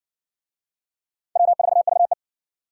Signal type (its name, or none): Morse code